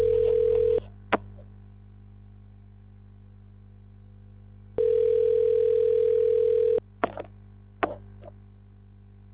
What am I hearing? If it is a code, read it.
Ringback tone (North America), 440 + 480 Hz.